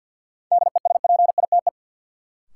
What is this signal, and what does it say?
Morse code, heard as a keyed tone, about 52 wpm, 700 Hz. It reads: DESPITE
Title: Sets of 4 Words - New Word Drill - 50wpm